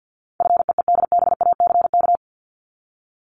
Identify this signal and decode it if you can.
Morse code, heard as a keyed tone, about 50 wpm, 700 Hz. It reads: FEEDBACK
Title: Top 1501-1750 Words 50wpm